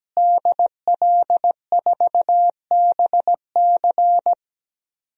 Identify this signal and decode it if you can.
Morse code, heard as a keyed tone, about 17 wpm, 700 Hz. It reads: DL4BC
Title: DXCC-100 Call Signs - 5 Characters - Repeated 2x - 17wpm